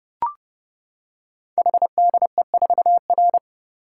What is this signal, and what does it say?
Morse code, heard as a keyed tone, about 30 wpm, 700 Hz. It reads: HDE4R
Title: Code Groups - 5 Letter:Numbers - 30wpm